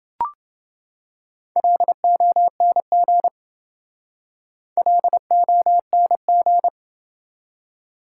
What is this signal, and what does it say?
Morse code, heard as a keyed tone, about 28 wpm, 700 Hz. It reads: LONG LONG